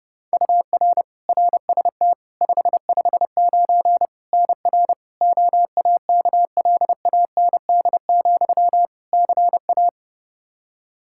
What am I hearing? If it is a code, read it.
Morse code, heard as a keyed tone, about 30 wpm, 700 Hz. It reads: UR RST 559 NR OAKLAND, CA
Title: K1USN SST QSO Practice 30wpm